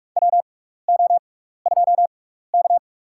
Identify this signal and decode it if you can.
Morse code, heard as a keyed tone, about 48 wpm, spaced as for 16 wpm, 700 Hz. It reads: WO2K
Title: USA Call Signs - 4 Characters - 6x Character Spacing - 45wpm